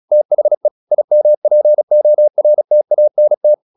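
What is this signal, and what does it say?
Morse code, heard as a keyed tone, about 36 wpm, 600 Hz. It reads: THE IMPORTANT
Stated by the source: Morse practice at 36 wpm, text of Ken Follett's novel Never, from CW med Gnister.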